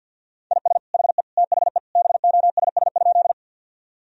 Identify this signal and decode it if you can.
Morse code, heard as a keyed tone, about 50 wpm, 700 Hz. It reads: IS HE THE BOSS?
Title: Sentences from Top 1000 Words - Review Part 4 - 50wpm